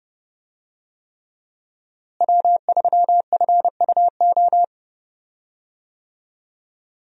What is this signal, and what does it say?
Morse code, heard as a keyed tone, about 28 wpm, 700 Hz. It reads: W3FUO
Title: USA Call Signs - 5 Characters - Repeated 3x - 30wpm